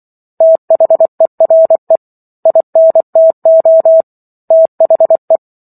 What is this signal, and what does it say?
Morse code, heard as a keyed tone, about 24 wpm, 650 Hz. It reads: THERE INTO THE
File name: Chapter 15 - 24 WPM